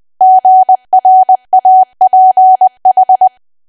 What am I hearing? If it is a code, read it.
Morse code, heard as a keyed tone, about 20 wpm, 750 Hz. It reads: GRAPH
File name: cwa / W1AW-Level3 / 20WPM